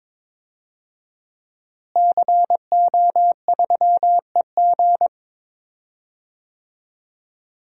Morse code, heard as a keyed tone, about 22 wpm, 700 Hz. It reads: CO3EG
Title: DXCC-100 Call Signs - 5 Characters - Repeated 2x - 22wpm